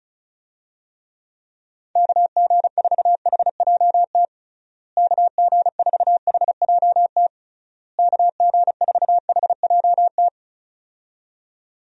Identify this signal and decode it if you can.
Morse code, heard as a keyed tone, about 34 wpm, 700 Hz. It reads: KG4HJT KG4HJT KG4HJT
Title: USA Call Signs - 6 Characters - Repeated 3x - 35wpm